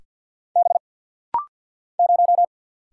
Morse code, heard as a keyed tone, about 50 wpm, 700 Hz. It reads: B 0
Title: Single Letter-Number - Rapid-Fire 50wpm